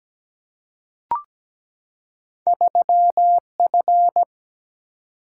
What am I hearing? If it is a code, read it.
Morse code, heard as a keyed tone, about 17 wpm, 700 Hz. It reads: 3F
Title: Single-Letter Single-Number 17wpm